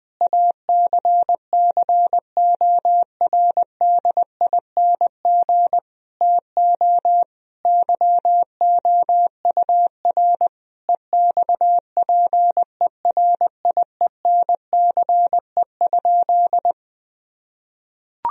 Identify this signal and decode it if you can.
Morse code, heard as a keyed tone, about 21 wpm, 700 Hz. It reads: ACCORDING TO YOUR EXPERIENCE?
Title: Sentences from Top 1000 Words - Review Part 4 - 20wpm